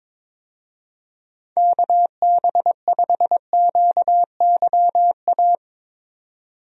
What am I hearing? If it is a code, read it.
Morse code, heard as a keyed tone, about 22 wpm, 700 Hz. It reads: KB5QYA
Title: USA Call Signs - 6 Characters - 22wpm